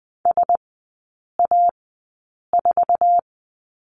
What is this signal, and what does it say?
Morse code, heard as a keyed tone, about 20 wpm, spaced as for 8 wpm, 700 Hz. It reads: SA4
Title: 047 Lesson 9e - Number 4 - Three Characters 20wpm